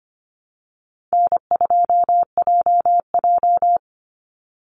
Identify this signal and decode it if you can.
Morse code, heard as a keyed tone, about 25 wpm, 700 Hz. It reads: N2JJ
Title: Call Signs - 25wpm